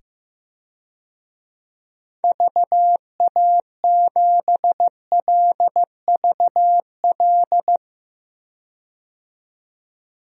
Morse code, heard as a keyed tone, about 15 wpm, 700 Hz. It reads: VA7LVL
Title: DXCC-100 Call Signs - 6 Characters - 15wpm